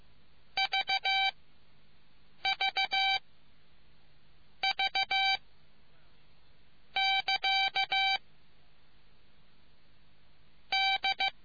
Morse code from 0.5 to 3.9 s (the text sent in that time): V V